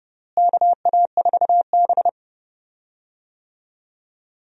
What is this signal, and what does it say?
Morse code, heard as a keyed tone, about 30 wpm, 700 Hz. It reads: KA4B
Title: USA Call Signs - 4 Characters - Repeated 3x - 30wpm